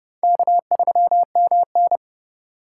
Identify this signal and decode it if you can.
Morse code, heard as a keyed tone, about 32 wpm, 700 Hz. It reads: K3MN